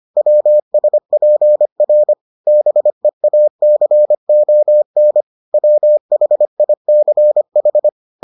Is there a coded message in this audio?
Yes: Morse code from 0.2 to 7.9 s, heard as a keyed tone, about 25 wpm, 600 Hz. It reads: WSPR BEACON WHICH